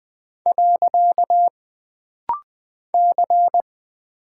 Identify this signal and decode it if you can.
Morse code, heard as a keyed tone, about 20 wpm, 700 Hz. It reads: . C
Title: Single Character (Letter:Number:Punctuation:Prosign) - Rapid-Fire - 20wpm